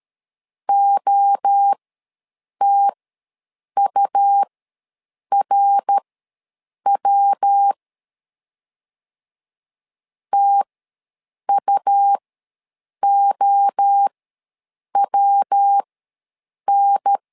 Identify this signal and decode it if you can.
Morse code, heard as a keyed tone, about 13 wpm, spaced as for 7 wpm, 800 Hz. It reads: OTURW TUOWN
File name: LEZIONI / 3 MWRO / 3 MWRO ws 12wpm 1914